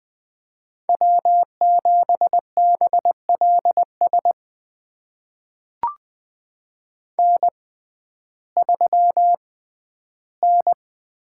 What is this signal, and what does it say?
Morse code, heard as a keyed tone, about 20 wpm, spaced as for 12 wpm, 700 Hz. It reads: W7BLS N3N
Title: USA Call Signs - 5 Characters - 6x Character Spacing - 20wpm